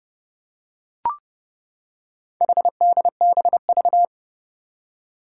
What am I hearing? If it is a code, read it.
Morse code, heard as a keyed tone, about 30 wpm, 690 Hz. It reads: HDBV
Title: Four Challenging Letter-Number Combinations 30wpm